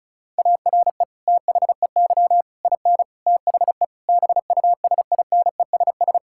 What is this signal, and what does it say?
Morse code, heard as a keyed tone, about 36 wpm, 700 Hz. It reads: ARE THEY IN THE BUSINESS
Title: Sentences from Top 1600 Words 35wpm